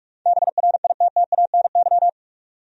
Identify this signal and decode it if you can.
Morse code, heard as a keyed tone, about 46 wpm, 700 Hz. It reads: BRITTANY
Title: USA Common Names - 45wpm